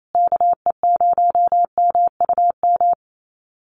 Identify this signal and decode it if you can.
Morse code, heard as a keyed tone, about 28 wpm, 700 Hz. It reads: KE0MUM
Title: USA Call Signs Repeated 3x 28wpm